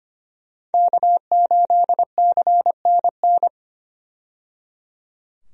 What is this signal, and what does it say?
Morse code, heard as a keyed tone, about 25 wpm, 700 Hz. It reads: K8CNN
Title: USA Call Signs - 5 Characters - 25wpm